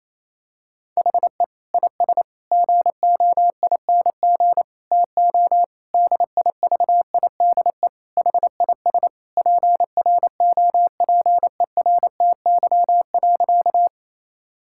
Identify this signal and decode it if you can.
Morse code, heard as a keyed tone, about 27 wpm, 700 Hz. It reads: HE IS GOING TO DIVIDE HIS PROPERTY.
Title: Sentences from Top 1600 Words 28wpm